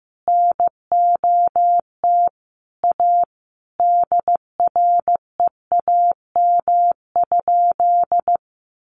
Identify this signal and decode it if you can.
Morse code, heard as a keyed tone, about 15 wpm, 700 Hz. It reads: NOT A DREAM?
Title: Sentences from Top 1500 Words 15wpm